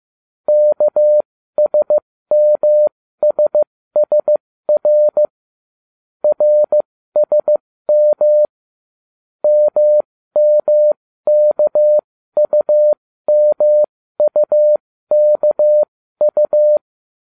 Morse code, heard as a keyed tone, about 15 wpm, spaced as for 13 wpm, 600 Hz. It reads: KSMSSR RSM MMKUMUKU